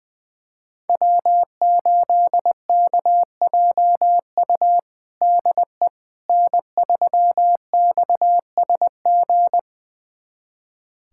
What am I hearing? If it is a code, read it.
Morse code, heard as a keyed tone, about 20 wpm, 700 Hz. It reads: W8KJU DE N3XSG
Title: USA Callsign DE Callsign - 5 Characters - 20wpm